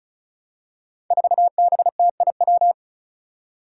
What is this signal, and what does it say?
Morse code, heard as a keyed tone, about 36 wpm, 700 Hz. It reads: 4BTIW
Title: Five Challenging Letter-Number Combinations 35wpm